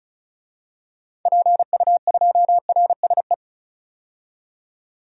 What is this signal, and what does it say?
Morse code, heard as a keyed tone, about 36 wpm, 700 Hz. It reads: PU2RSE
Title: DXCC-100 Call Signs - 6 Characters - 35wpm